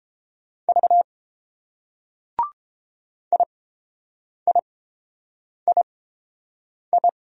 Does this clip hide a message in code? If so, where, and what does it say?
Yes: Morse code from 0.7 to 7.1 s, heard as a keyed tone, about 29 wpm, spaced as for 7 wpm, 690 Hz. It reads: V IIII